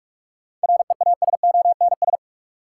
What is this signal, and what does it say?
Morse code, heard as a keyed tone, about 42 wpm, 700 Hz. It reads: REASONS
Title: Sets of 4 Words - New Word Drill - 45wpm